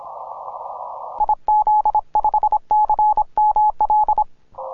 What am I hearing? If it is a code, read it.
Morse code, heard as a keyed tone, about 26 wpm, 850 Hz. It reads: IZ5CML
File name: simple-file-list / tx5k 10mcw